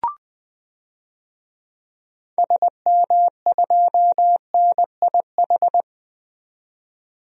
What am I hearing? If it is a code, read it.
Morse code, heard as a keyed tone, about 20 wpm, 700 Hz. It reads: SM2NIH
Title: DXCC-100 Call Signs - 6 Characters - 20wpm